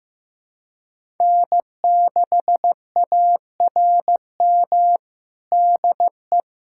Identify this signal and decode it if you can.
Morse code, heard as a keyed tone, about 16 wpm, 700 Hz. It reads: N6ARM DE
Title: USA Callsign DE Callsign - 5 Characters - 15wpm